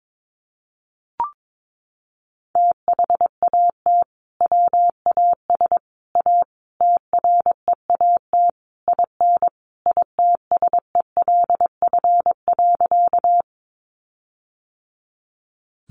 Morse code, heard as a keyed tone, about 22 wpm, 700 Hz. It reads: THAT WAS A TREAT IN ITSELF.
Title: Sentences from Top 1300 Words 22wpm